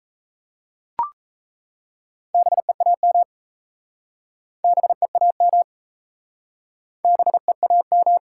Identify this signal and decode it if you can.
Morse code, heard as a keyed tone, about 37 wpm, 700 Hz. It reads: BEAM BEAM BEAM